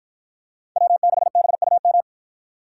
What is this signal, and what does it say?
Morse code, heard as a keyed tone, about 53 wpm, 700 Hz. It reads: W6BUM